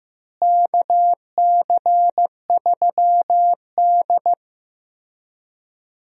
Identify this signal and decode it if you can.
Morse code, heard as a keyed tone, about 16 wpm, 700 Hz. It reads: KC3D
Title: USA Call Signs - 4 Characters - Repeated 3x - 15wpm